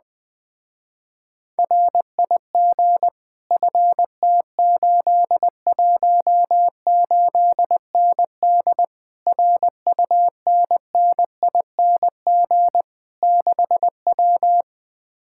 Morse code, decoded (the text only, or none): RIG FT818ND RUNNING 6W